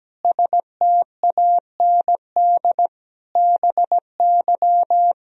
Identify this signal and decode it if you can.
Morse code, heard as a keyed tone, about 17 wpm, 700 Hz. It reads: STAND BY
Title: Sentences from Top 1000 Words - Review Part 1 - 17wpm